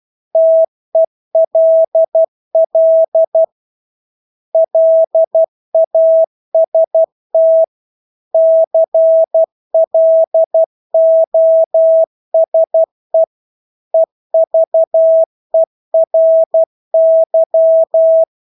Morse code, heard as a keyed tone, about 12 wpm, 650 Hz. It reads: TELL LAST CLOSE EVERY
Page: Engelske ord 12 wpm